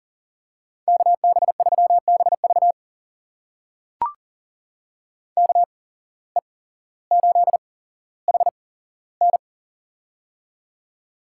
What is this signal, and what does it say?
Morse code, heard as a keyed tone, about 40 wpm, spaced as for 17 wpm, 700 Hz. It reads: KB3BV KE8HN
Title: USA Call Signs - 5 Characters - 8x Character Spacing - 40wpm